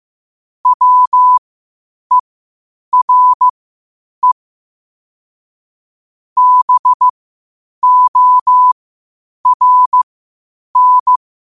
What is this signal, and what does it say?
Morse code, heard as a keyed tone, about 15 wpm, spaced as for 8 wpm, 1000 Hz. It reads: WERE BORN
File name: qotdfiles / 8wpm